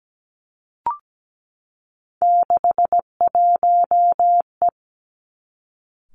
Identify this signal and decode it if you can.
Morse code, heard as a keyed tone, about 17 wpm, 700 Hz. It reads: 61E
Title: Three Challenging Letter-Number Combinations 17wpm